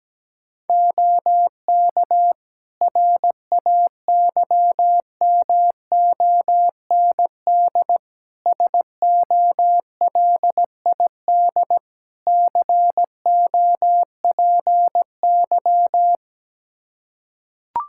Morse code, heard as a keyed tone, about 17 wpm, 700 Hz. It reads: OK RAYMOND SOLID COPY
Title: Practice QSO - Exchange 3 - More Info - 17wpm